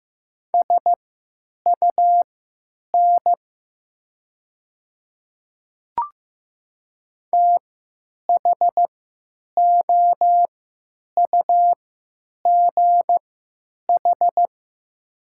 Morse code, heard as a keyed tone, about 15 wpm, spaced as for 8 wpm, 700 Hz. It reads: SUN THOUGH
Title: 3x Character Spacing - Top 500 Words - 15wpm